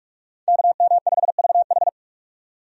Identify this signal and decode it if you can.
Morse code, heard as a keyed tone, about 42 wpm, 700 Hz. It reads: KM5VH
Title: USA Call Signs - 5 Characters - 45wpm